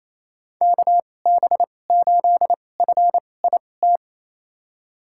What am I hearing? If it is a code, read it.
Morse code, heard as a keyed tone, about 28 wpm, spaced as for 20 wpm, 700 Hz. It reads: KB8FIT
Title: USA Call Signs - 6 Characters - 2x Character Spacing - 28wpm